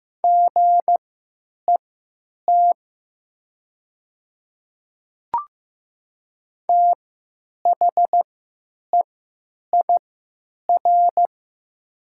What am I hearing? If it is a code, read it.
Morse code, heard as a keyed tone, about 15 wpm, spaced as for 8 wpm, 700 Hz. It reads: GET THEIR
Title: 3x Character Spacing - Top 200 Words - 15wpm